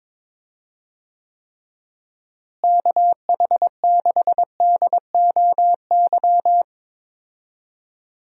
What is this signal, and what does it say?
Morse code, heard as a keyed tone, about 22 wpm, 700 Hz. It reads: KH6DOY
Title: DXCC-100 Call Signs - 6 Characters - Repeated 2x - 22wpm